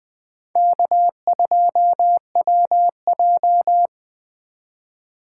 Morse code, heard as a keyed tone, about 20 wpm, 700 Hz. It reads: K2WJ
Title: USA Call Signs - 4 Characters - 4x Character Spacing - 20wpm